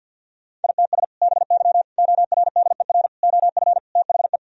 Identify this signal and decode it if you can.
Morse code, heard as a keyed tone, about 50 wpm, 700 Hz. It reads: ITS BY ORDER OF THE
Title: Sentences from Top 1000 Words - Review Part 3 - 50wpm